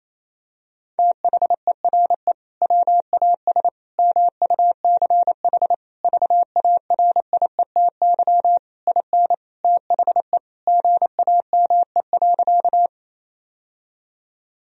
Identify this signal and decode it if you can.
Morse code, heard as a keyed tone, about 28 wpm, 700 Hz. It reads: THERE WAS MUCH VARIETY IN THE GAME.
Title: Sentences from Top 1000 Words - Review Part 4 - 28wpm